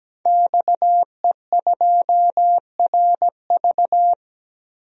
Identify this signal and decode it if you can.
Morse code, heard as a keyed tone, about 17 wpm, 700 Hz. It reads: XE2RV
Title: DXCC-100 Call Signs - 5 Characters - 17wpm